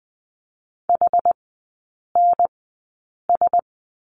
Morse code, heard as a keyed tone, about 20 wpm, spaced as for 8 wpm, 690 Hz. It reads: HNS
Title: 087 Lesson 15e - Number 5 - Three Characters 20wpm